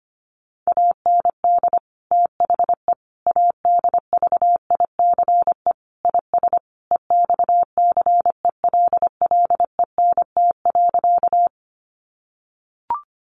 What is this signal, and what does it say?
Morse code, heard as a keyed tone, about 25 wpm, 700 Hz. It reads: AND THE ADVICE IS EXCELLENT.
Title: Sentences from Top 1100 Words 25wpm